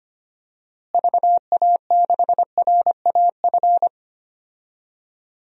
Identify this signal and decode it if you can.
Morse code, heard as a keyed tone, about 25 wpm, 700 Hz. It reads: VA6RAF
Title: DXCC-100 Call Signs - 6 Characters - Repeated 2x - 25wpm